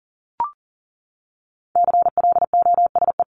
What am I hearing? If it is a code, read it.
Morse code, heard as a keyed tone, about 40 wpm, 700 Hz. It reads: CLOSE
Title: LY Suffix - Top 100 Words with Base Words 40wpm